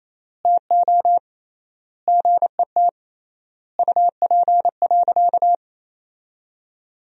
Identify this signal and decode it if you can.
Morse code, heard as a keyed tone, about 28 wpm, 700 Hz. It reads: TO GET UP.
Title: 3x Word-Spacing - Sentences from Top 500 Words - 28wpm